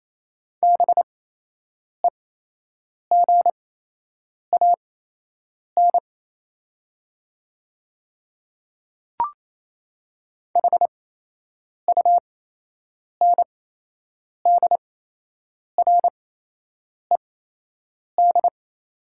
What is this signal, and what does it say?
Morse code, heard as a keyed tone, about 28 wpm, spaced as for 8 wpm, 700 Hz. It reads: BEGAN HUNDRED
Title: 8x Character Spacing - Top 500 Words - 28wpm